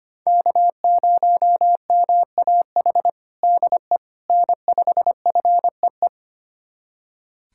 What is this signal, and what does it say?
Morse code, heard as a keyed tone, about 25 wpm, 700 Hz. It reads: K0MAH DE N5FEE
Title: USA Callsign DE Callsign 25wpm